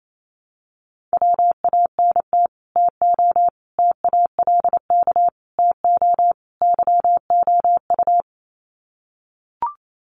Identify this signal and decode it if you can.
Morse code, heard as a keyed tone, about 28 wpm, 690 Hz. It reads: WANT TO TALK TO YOU
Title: Sets of 5 Words Encore 28wpm